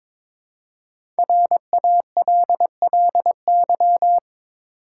Morse code, heard as a keyed tone, about 22 wpm, 700 Hz. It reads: RALLY
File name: 5 Letter Words - First 3 Letters Given - 22wpm